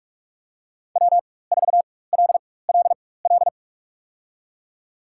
Morse code, heard as a keyed tone, about 42 wpm, spaced as for 20 wpm, 700 Hz. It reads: W4LLL
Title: USA Call Signs - 5 Characters - 4x Character Spacing - 45wpm